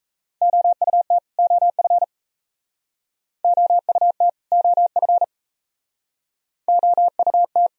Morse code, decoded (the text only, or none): OUT OF OUT OF OUT